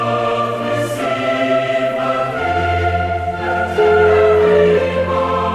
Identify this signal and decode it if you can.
Ringback tone (Europe, 425 Hz), with noise.